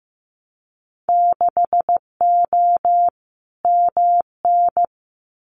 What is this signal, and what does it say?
Morse code, heard as a keyed tone, about 15 wpm, 700 Hz. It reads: 6O MN
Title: Winter Field Day Exchanges Encore 15wpm